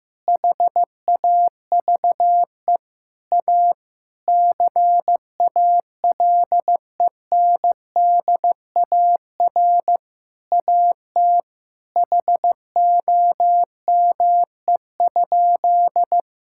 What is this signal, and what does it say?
Morse code, heard as a keyed tone, about 15 wpm, 700 Hz. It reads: HAVE A CALENDAR AT HOME?
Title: Sentences from Top 1600 Words 15wpm